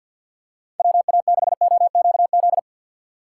Morse code, heard as a keyed tone, about 53 wpm, 700 Hz. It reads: WA6OQZ
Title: USA Call Signs - 6 Characters - 4x Character Spacing - 50wpm